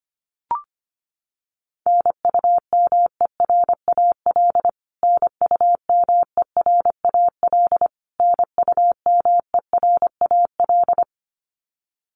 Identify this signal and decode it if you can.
Morse code, heard as a keyed tone, about 25 wpm, 700 Hz. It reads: NUMERAL NUMERAL NUMERAL
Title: Top 500 Words Repeated 25wpm